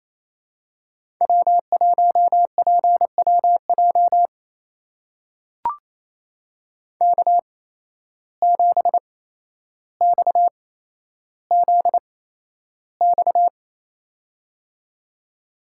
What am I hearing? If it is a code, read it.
Morse code, heard as a keyed tone, about 28 wpm, spaced as for 12 wpm, 700 Hz. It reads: W1PWJ K7XZX
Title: USA Call Signs - 5 Characters - 8x Character Spacing - 28wpm